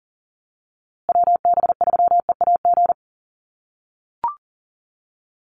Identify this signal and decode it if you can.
Morse code, heard as a keyed tone, about 40 wpm, 700 Hz. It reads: WB3EAG
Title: USA Call Signs Repeated 3x 40wpm